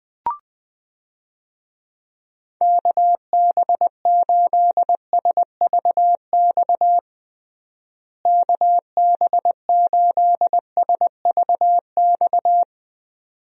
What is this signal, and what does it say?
Morse code, heard as a keyed tone, about 20 wpm, 700 Hz. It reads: KB8SVX KB8SVX
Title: USA Call Signs - 6 Characters - Repeated 2x - 20wpm